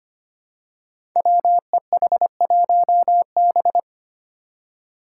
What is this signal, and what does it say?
Morse code, heard as a keyed tone, about 25 wpm, 700 Hz. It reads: WEH1B
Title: Five Challenging Letter-Number Combinations 25wpm